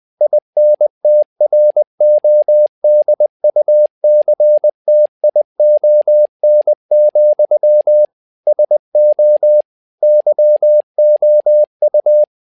Morse code, heard as a keyed tone, about 20 wpm, 600 Hz. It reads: INTRODUCTION, SO YOU
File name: learning center / tower trouble / tower trouble 20wpm